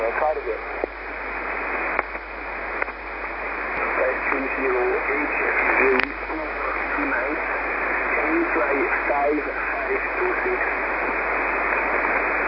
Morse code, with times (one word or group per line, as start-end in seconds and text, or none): none